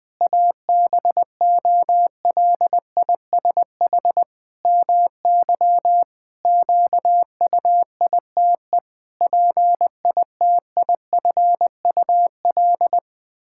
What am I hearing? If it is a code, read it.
Morse code, heard as a keyed tone, about 20 wpm, 700 Hz. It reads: ABOLISH MY QUITE PITIFUL
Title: Pangram Challenge v2 - 20wpm